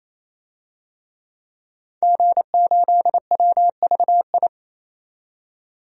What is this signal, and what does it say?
Morse code, heard as a keyed tone, about 28 wpm, 700 Hz. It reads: G8WVI
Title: DXCC-100 Call Signs - 5 Characters - 28wpm